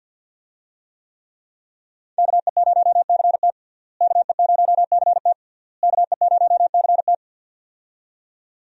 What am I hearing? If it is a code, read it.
Morse code, heard as a keyed tone, about 46 wpm, 700 Hz. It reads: KE0CT KE0CT KE0CT
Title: USA Call Signs - 5 Characters - Repeated 3x - 50wpm